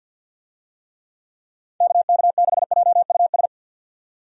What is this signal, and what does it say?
Morse code, heard as a keyed tone, about 48 wpm, 700 Hz. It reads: KK6JUS
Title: USA Call Signs - 6 Characters - 50wpm